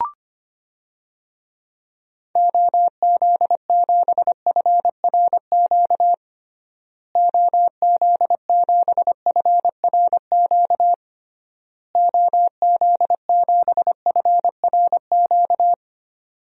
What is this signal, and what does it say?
Morse code, heard as a keyed tone, about 25 wpm, 700 Hz. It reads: OZ7FRQ OZ7FRQ OZ7FRQ
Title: DXCC-100 Call Signs - 6 Characters - Repeated 3x - 25wpm